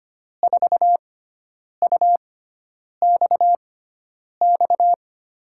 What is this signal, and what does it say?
Morse code, heard as a keyed tone, about 25 wpm, spaced as for 9 wpm, 700 Hz. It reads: 4UXX
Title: USA Call Signs - 5 Characters - 6x Character Spacing - 25wpm